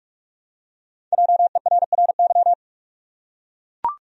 Morse code, heard as a keyed tone, about 43 wpm, 700 Hz. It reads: JERRY
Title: USA Common Names - 45wpm